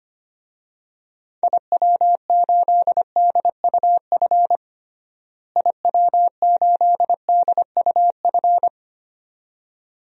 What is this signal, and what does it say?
Morse code, heard as a keyed tone, about 25 wpm, 700 Hz. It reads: IW8DUF IW8DUF
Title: DXCC-100 Call Signs - 6 Characters - Repeated 2x - 25wpm